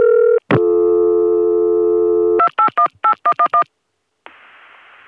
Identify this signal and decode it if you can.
DTMF, heard as telephone touch tones, tones of about 90 ms. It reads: 2818111